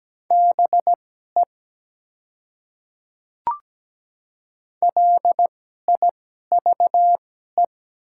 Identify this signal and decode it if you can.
Morse code, heard as a keyed tone, about 17 wpm, spaced as for 12 wpm, 700 Hz. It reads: BE LIVE
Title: 2x Character Spacing - Top 200 Words - 17wpm